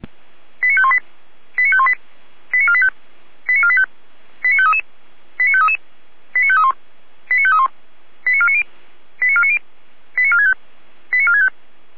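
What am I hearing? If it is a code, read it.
Five-tone selective call (ZVEI), heard as quick runs of five tones, tones of about 80 ms. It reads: 89518 89518 89485 89485 89533 89533 89531 89531 89490 89490 89465 89465
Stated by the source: ZVEI five-tone selective call received on an SDR.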